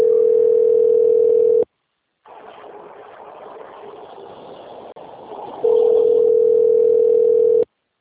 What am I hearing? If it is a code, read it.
Ringback tone (North America), 440 + 480 Hz.